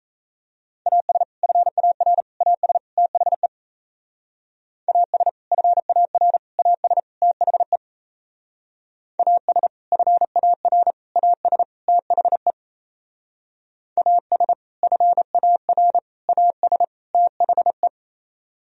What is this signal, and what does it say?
Morse code, heard as a keyed tone, about 36 wpm, 700 Hz. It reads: AS FAR AS THE AS FAR AS THE AS FAR AS THE AS FAR AS THE